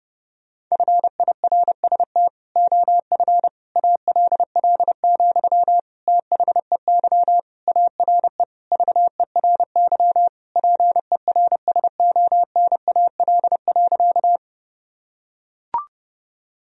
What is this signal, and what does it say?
Morse code, heard as a keyed tone, about 30 wpm, 700 Hz. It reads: FIRST OF ALL, THEY ARE VERY PERSONAL.
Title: Sentences from Top 1000 Words - Review Part 4 - 30wpm